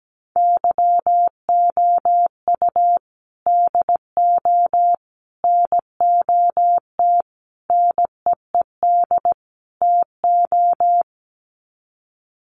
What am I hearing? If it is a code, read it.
Morse code, heard as a keyed tone, about 17 wpm, 700 Hz. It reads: YOU DO NOT NEED TO
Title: Sets of 5 Words 17wpm